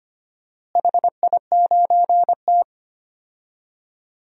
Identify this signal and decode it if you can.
Morse code, heard as a keyed tone, about 25 wpm, 700 Hz. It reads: HI9T
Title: DXCC-100 Call Signs - 4 Characters - Repeated 2x - 25wpm